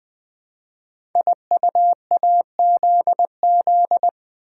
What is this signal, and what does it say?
Morse code, heard as a keyed tone, about 20 wpm, 700 Hz. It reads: IUAZZ
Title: Code Groups - 5 Letters - 20wpm